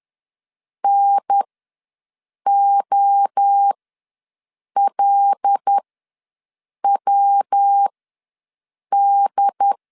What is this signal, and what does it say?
Morse code, heard as a keyed tone, about 10 wpm, spaced as for 6 wpm, 800 Hz. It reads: NOLWD
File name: LEZIONI / 5 LBVC / 5 L ws 10wpm 3636